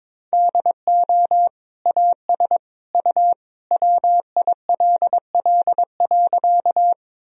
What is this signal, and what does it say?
Morse code, heard as a keyed tone, about 22 wpm, 700 Hz. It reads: DO AS U WILL.
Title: Sentences from Top 100 Words and Abbreviations 22wpm